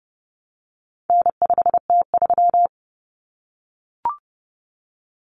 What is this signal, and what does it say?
Morse code, heard as a keyed tone, about 30 wpm, 700 Hz. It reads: N5T3
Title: Common Cut Numbers - 4-Digit 30wpm